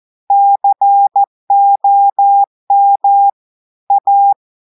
Morse code, heard as a keyed tone, about 14 wpm, 800 Hz. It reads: COM A
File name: audios / portugais / portugais texte en clair 1 vitesse 14 mots minutes